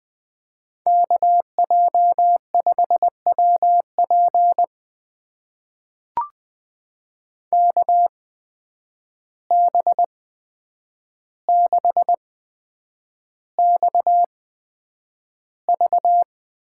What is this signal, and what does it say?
Morse code, heard as a keyed tone, about 20 wpm, spaced as for 9 wpm, 700 Hz. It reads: KJ5WP KB6XV
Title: USA Call Signs - 5 Characters - 8x Character Spacing - 20wpm